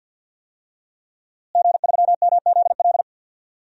Morse code, heard as a keyed tone, about 49 wpm, 700 Hz. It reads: G3MZL
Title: DXCC-100 Call Signs - 5 Characters - Repeated 3x - 50wpm